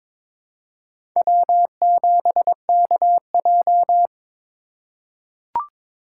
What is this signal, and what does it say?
Morse code, heard as a keyed tone, about 22 wpm, 700 Hz. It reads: W7KJ
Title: USA Call Signs - 4 Characters - 3x Character Spacing - 22wpm